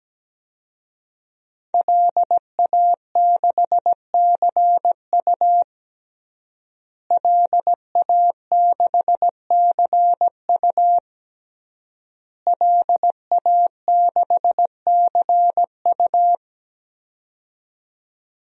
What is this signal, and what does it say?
Morse code, heard as a keyed tone, about 17 wpm, 700 Hz. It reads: LA6CU LA6CU LA6CU
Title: DXCC-100 Call Signs - 5 Characters - Repeated 3x - 17wpm